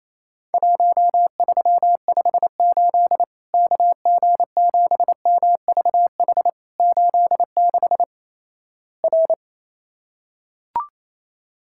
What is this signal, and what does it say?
Morse code, heard as a keyed tone, about 28 wpm, 700 Hz. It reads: 1358 KG7MVH 86 R
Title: Worked All Europe DX Contest - QTC Exchange - 28wpm